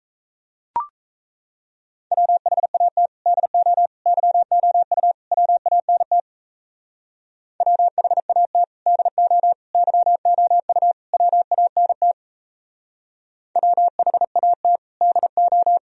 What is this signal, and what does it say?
Morse code, heard as a keyed tone, about 35 wpm, 700 Hz. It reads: WHAT DO YOU WANT WHAT DO YOU WANT WHAT DO
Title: Speed-Racing - Sets of 4 Words 28wpm